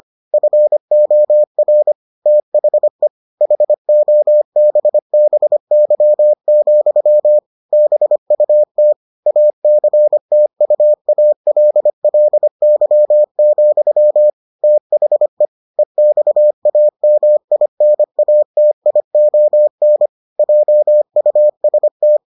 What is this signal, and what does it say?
Morse code, heard as a keyed tone, about 25 wpm, 600 Hz. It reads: FOR THE HOBBY, BUT ACTUALLY, THE EXAMINATION JUST